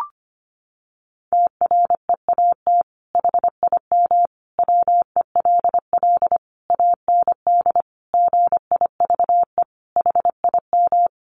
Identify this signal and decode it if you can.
Morse code, heard as a keyed tone, about 25 wpm, 700 Hz. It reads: TREAT HIM WELL AND GIVE HIM
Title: Sentences from Top 1300 Words 25wpm